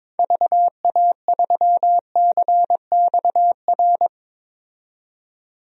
Morse code, heard as a keyed tone, about 22 wpm, 700 Hz. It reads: VA3CXR
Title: DXCC-100 Call Signs - 6 Characters - 22wpm